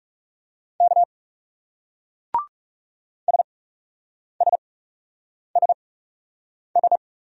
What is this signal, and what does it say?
Morse code, heard as a keyed tone, about 38 wpm, spaced as for 8 wpm, 700 Hz. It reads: K SSSS